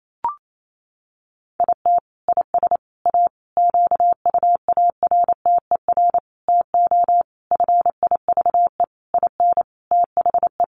Morse code, heard as a keyed tone, about 28 wpm, 700 Hz. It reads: IT IS A QUARTER TO FIVE IN THE
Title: Sentences from Top 1300 Words 28wpm